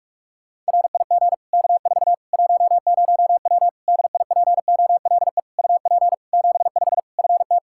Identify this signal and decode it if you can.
Morse code, heard as a keyed tone, about 47 wpm, 700 Hz. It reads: RIG K4 10W DIPOLE UP 75 FT
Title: Practice QSO - Exchange 2 - Info - 45wpm